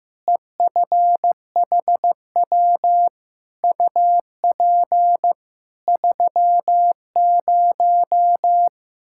Morse code, heard as a keyed tone, about 15 wpm, 700 Hz. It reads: EFHW UP 30
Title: Practice QSO - Exchange 2 - Info - 15wpm